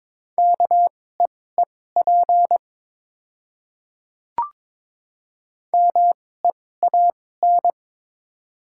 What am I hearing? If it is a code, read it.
Morse code, heard as a keyed tone, about 21 wpm, spaced as for 16 wpm, 700 Hz. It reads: KEEP MEAN
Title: 2x Character Spacing - Top 300 Words - 22wpm